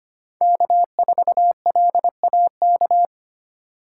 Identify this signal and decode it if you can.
Morse code, heard as a keyed tone, about 25 wpm, 700 Hz. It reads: K4LAK
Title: USA Call Signs - 5 Characters - 3x Character Spacing - 25wpm